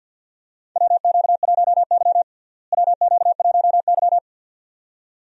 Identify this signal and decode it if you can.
Morse code, heard as a keyed tone, about 53 wpm, 690 Hz. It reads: WQ1Y WQ1Y
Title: USA Call Signs - 4 Characters - Repeated 2x - 50wpm